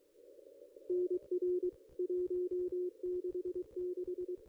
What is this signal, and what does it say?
Morse code, heard as a keyed tone, about 23 wpm, 360 Hz. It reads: NR 166
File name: contests / ss-cw-k5tr-op-n5rz -1900